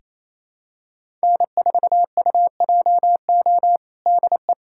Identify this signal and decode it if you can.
Morse code, heard as a keyed tone, about 28 wpm, 700 Hz. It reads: N4UJO DE
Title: USA Callsign DE Callsign - 5 Characters - 28wpm